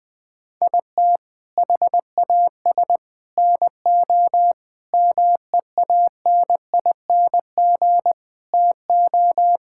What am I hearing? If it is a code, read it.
Morse code, heard as a keyed tone, about 19 wpm, 700 Hz. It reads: IT HAS NO MEANING TO
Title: Sentences from Top 1000 Words - Review Part 6 - 20wpm